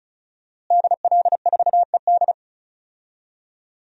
Morse code, heard as a keyed tone, about 36 wpm, 700 Hz. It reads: DL4ED
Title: DXCC-100 Call Signs - 5 Characters - Repeated 2x - 35wpm